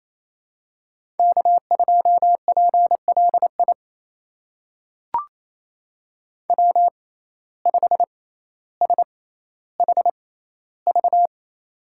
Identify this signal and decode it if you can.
Morse code, heard as a keyed tone, about 28 wpm, spaced as for 14 wpm, 700 Hz. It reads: K2PLI W5SHV
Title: USA Call Signs - 5 Characters - 6x Character Spacing - 28wpm